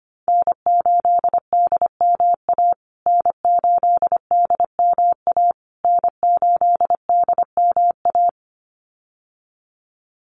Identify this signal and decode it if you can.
Morse code, heard as a keyed tone, about 25 wpm, 700 Hz. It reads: N8DMA N8DMA N8DMA
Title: USA Call Signs Repeated 3x 25wpm